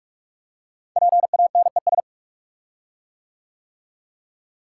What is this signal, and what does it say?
Morse code, heard as a keyed tone, about 44 wpm, 700 Hz. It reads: PANES
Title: Code Groups - 5 Letters - 45wpm